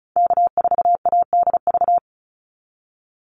Morse code, heard as a keyed tone, about 34 wpm, 700 Hz. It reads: K4ADV
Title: NA SOTA Activator Callsigns 35wpm